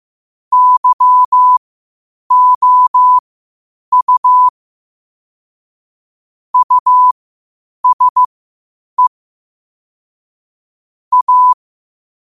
Morse code, heard as a keyed tone, about 15 wpm, spaced as for 8 wpm, 1000 Hz. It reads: YOU USE A
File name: qotdfiles / 8wpm